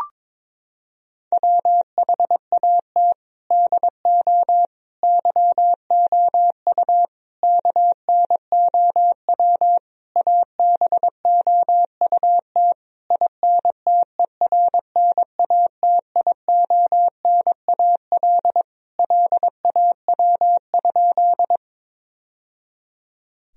Morse code, heard as a keyed tone, about 22 wpm, 700 Hz. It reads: WHAT DO YOU KNOW ABOUT INTERNATIONAL LAW?
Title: Sentences from Top 1000 Words - Review Part 4 - 22wpm